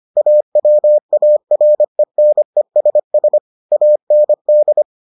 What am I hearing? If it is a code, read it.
Morse code, heard as a keyed tone, about 25 wpm, 600 Hz. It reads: AWARENESS AND